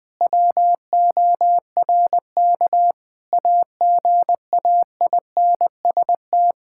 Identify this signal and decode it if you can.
Morse code, heard as a keyed tone, about 20 wpm, 700 Hz. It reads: WORK AGAINST
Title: Sentences from Top 1000 Words - Review Part 5 - 20wpm